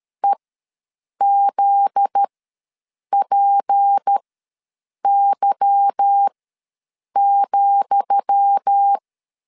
Morse code, heard as a keyed tone, about 13 wpm, spaced as for 7 wpm, 800 Hz. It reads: EZPY,
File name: LEZIONI / 10 SIGNS / 10D COMA ws 12wpm 3618